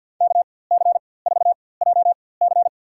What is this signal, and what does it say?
Morse code, heard as a keyed tone, about 53 wpm, spaced as for 24 wpm, 700 Hz. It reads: KC4JC
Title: USA Call Signs - 4x Character Spacing - 50wpm